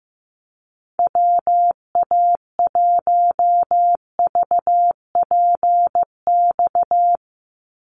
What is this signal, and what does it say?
Morse code, heard as a keyed tone, about 15 wpm, 700 Hz. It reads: WA1VPX
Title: USA Call Signs 15wpm